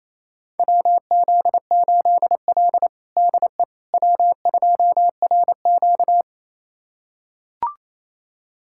Morse code, heard as a keyed tone, about 28 wpm, 700 Hz. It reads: WZ8L DE W2RQ
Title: USA Callsign DE Callsign - 4 Characters - 28wpm